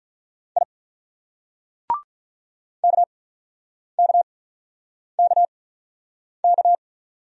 Morse code, heard as a keyed tone, about 48 wpm, spaced as for 9 wpm, 700 Hz. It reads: I KKKK